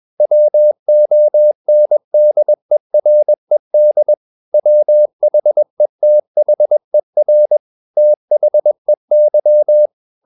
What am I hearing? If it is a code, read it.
Morse code, heard as a keyed tone, about 21 wpm, 600 Hz. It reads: WONDERED WHETHER THEY
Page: CW med Gnister